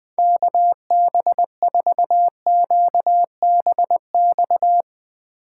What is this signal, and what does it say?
Morse code, heard as a keyed tone, about 21 wpm, 700 Hz. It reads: KB4QBX